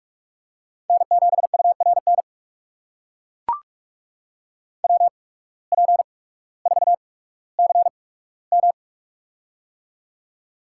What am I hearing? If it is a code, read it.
Morse code, heard as a keyed tone, about 42 wpm, spaced as for 19 wpm, 700 Hz. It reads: N7URN WP4CM
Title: USA Call Signs - 5 Characters - 8x Character Spacing - 45wpm